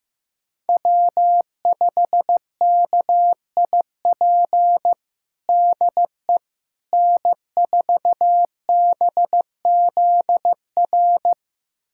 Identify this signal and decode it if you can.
Morse code, heard as a keyed tone, about 15 wpm, 700 Hz. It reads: W5KIP DE N4BZR
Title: USA Callsign DE Callsign - 5 Characters - 15wpm